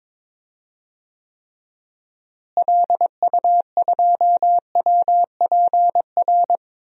Morse code, heard as a keyed tone, about 22 wpm, 700 Hz. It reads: LU2WPR
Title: DXCC-100 Call Signs - 6 Characters - Repeated 2x - 22wpm